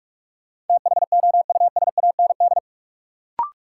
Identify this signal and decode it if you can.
Morse code, heard as a keyed tone, about 46 wpm, 700 Hz. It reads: THOUSAND
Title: Sentences from Top 600 Words - New Word Drill - 45wpm